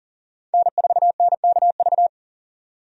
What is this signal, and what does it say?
Morse code, heard as a keyed tone, about 40 wpm, 700 Hz. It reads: N4NKV
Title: USA Call Signs - 5 Characters - 40wpm